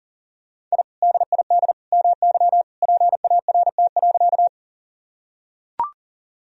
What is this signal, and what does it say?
Morse code, heard as a keyed tone, about 37 wpm, 700 Hz. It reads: I DID MY PART.
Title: Sentences from Top 1000 Words - Review Part 1 - 40wpm